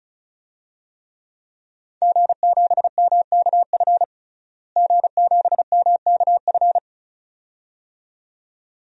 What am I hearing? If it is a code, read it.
Morse code, heard as a keyed tone, about 36 wpm, 700 Hz. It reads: G7MKF G7MKF
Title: DXCC-100 Call Signs - 5 Characters - Repeated 2x - 35wpm